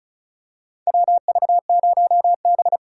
Morse code, heard as a keyed tone, about 36 wpm, 700 Hz. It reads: WV0B
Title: USA Call Signs - 4 Characters - 4x Character Spacing - 35wpm